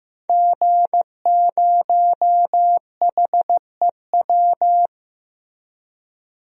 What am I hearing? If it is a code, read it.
Morse code, heard as a keyed tone, about 15 wpm, 700 Hz. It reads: G0HEW
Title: DXCC-100 Call Signs - 5 Characters - Repeated 3x - 15wpm